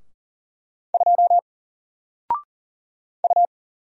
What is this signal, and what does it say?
Morse code, heard as a keyed tone, about 40 wpm, 700 Hz. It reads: 2 U
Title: Single Letter-Number Alt 40wpm